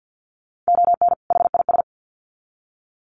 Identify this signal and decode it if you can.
Morse code, heard as a keyed tone, about 50 wpm, 700 Hz. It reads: ON HIS
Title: Sets of 2 Words 50wpm